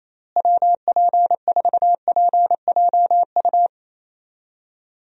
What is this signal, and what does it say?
Morse code, heard as a keyed tone, about 27 wpm, 700 Hz. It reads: WP4PJU
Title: USA Call Signs - 6 Characters - Repeated 2x - 28wpm